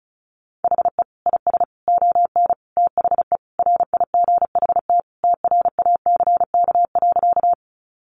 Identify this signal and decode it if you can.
Morse code, heard as a keyed tone, about 34 wpm, 700 Hz. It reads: HE IS ON THE RIGHT TRACK.
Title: Sentences from Top 1100 Words 35wpm